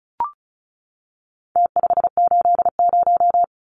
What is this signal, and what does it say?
Morse code, heard as a keyed tone, about 34 wpm, 700 Hz. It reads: T580
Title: Numbers - Contest Style - 4-Digit 35wpm